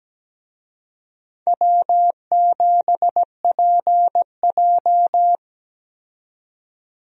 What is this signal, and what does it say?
Morse code, heard as a keyed tone, about 17 wpm, 700 Hz. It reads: W7PJ
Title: USA Call Signs - 4 Characters - 17wpm